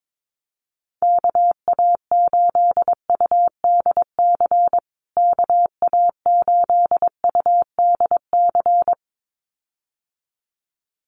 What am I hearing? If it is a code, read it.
Morse code, heard as a keyed tone, about 22 wpm, 700 Hz. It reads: KA8UDC KA8UDC
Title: USA Call Signs Repeated 2x 22wpm